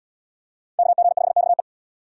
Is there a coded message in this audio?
Yes: Morse code from 0.8 to 1.6 s, heard as a keyed tone, about 68 wpm, 700 Hz. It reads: KG5XE